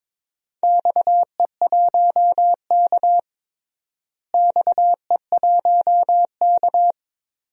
Morse code, heard as a keyed tone, about 22 wpm, 700 Hz. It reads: XE1K XE1K
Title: DXCC-100 Call Signs - 4 Characters - Repeated 2x - 22wpm